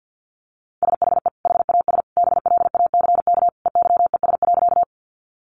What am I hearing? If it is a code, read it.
Morse code, heard as a keyed tone, about 50 wpm, 700 Hz. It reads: SHE HAS BLACK EYES.
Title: Easy Phrases 50wpm